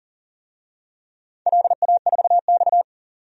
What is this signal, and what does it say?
Morse code, heard as a keyed tone, about 40 wpm, 700 Hz. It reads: LA4X
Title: DXCC-100 Call Signs - 4 Characters - 40wpm